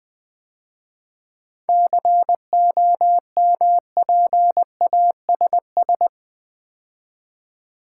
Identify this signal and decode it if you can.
Morse code, heard as a keyed tone, about 21 wpm, 700 Hz. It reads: COMPASS
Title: Difficult Words - Speed-Racing with Hints - 20wpm